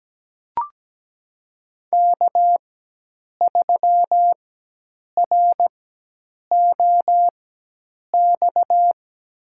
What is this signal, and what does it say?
Morse code, heard as a keyed tone, about 17 wpm, spaced as for 8 wpm, 700 Hz. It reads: K3ROX
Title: USA Call Signs - 5 Characters - 4x Character Spacing - 17wpm